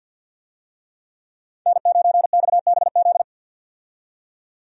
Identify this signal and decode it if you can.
Morse code, heard as a keyed tone, about 50 wpm, 700 Hz. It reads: N9XBZ
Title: USA Call Signs - 5 Characters - 50wpm